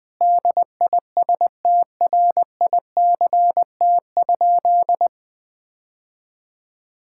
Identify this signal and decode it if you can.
Morse code, heard as a keyed tone, about 21 wpm, 700 Hz. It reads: DISTRICT?
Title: Sentences from Top 1600 Words 20wpm